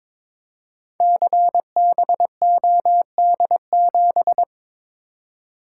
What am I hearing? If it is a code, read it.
Morse code, heard as a keyed tone, about 22 wpm, 700 Hz. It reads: CBOD7
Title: Code Groups - 5 Letter:Numbers - 22wpm